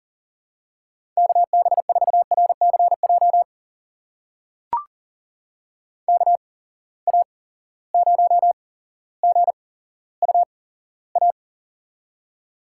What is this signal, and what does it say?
Morse code, heard as a keyed tone, about 39 wpm, spaced as for 17 wpm, 700 Hz. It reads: KB4RCJ KA0GUA
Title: USA Call Signs - 6 Characters - 8x Character Spacing - 40wpm